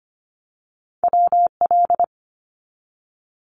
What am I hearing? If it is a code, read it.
Morse code, heard as a keyed tone, about 25 wpm, 700 Hz. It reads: WL
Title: Abbreviations 25wpm